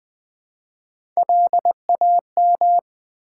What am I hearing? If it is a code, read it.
Morse code, heard as a keyed tone, about 20 wpm, 700 Hz. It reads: LAM
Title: Build A Word - 1 Letter At A Time - 4-Letter Words - 20wpm